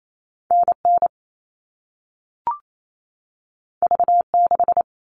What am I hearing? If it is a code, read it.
Morse code, heard as a keyed tone, about 28 wpm, 700 Hz. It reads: NN V6
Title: All Cut Numbers - 2-Digit 28wpm